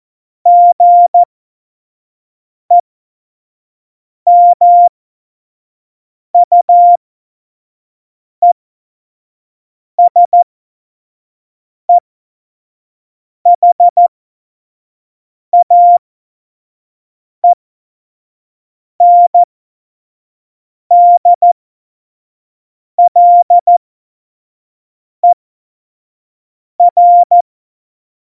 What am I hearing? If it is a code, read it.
Morse code, heard as a keyed tone, about 14 wpm, spaced as for 5 wpm, 700 Hz. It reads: GEMUESEHAENDLER
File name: Morsekurs / Witze / Witz 18. Februar 2020, 25 F70